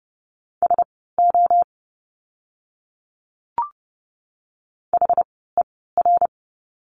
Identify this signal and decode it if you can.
Morse code, heard as a keyed tone, about 30 wpm, spaced as for 17 wpm, 700 Hz. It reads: SO HER
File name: course / 30:10wpm / 230 Lesson 37f - Letter x - Words 30wpm